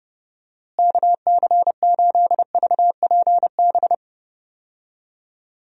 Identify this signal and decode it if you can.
Morse code, heard as a keyed tone, about 30 wpm, 700 Hz. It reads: KC8VPB
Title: USA Call Signs - 6 Characters - Repeated 2x - 30wpm